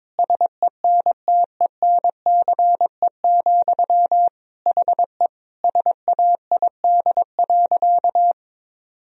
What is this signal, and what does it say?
Morse code, heard as a keyed tone, about 22 wpm, 700 Hz. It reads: SENTENCE, HE SAID.
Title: Sentences from Top 1000 Words - Review Part 1 - 22wpm